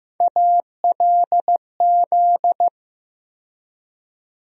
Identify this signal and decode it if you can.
Morse code, heard as a keyed tone, about 15 wpm, 700 Hz. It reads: ALZ
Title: USA Call Sign Postfixes - 3 Letters 15wpm